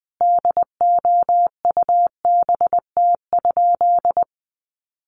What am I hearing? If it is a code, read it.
Morse code, heard as a keyed tone, about 20 wpm, 700 Hz. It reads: DOUBT?
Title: Sentences from Top 1500 Words 20wpm